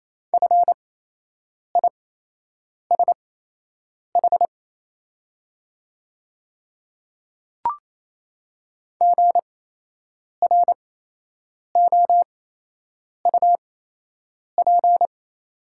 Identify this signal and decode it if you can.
Morse code, heard as a keyed tone, about 28 wpm, spaced as for 8 wpm, 700 Hz. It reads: FISH GROUP
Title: 8x Character Spacing - Top 300 Words - 28wpm